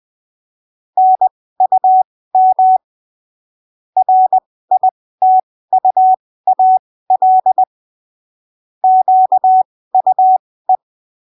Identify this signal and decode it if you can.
Morse code, heard as a keyed tone, about 20 wpm, spaced as for 15 wpm, 750 Hz. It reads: NUM RITUAL QUE
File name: arquivos / texto / texto 08 20ppm